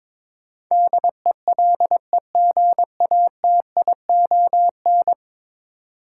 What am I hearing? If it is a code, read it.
Morse code, heard as a keyed tone, about 22 wpm, 700 Hz. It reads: DELEGATION
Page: morse-code-ninja-practice-22wpm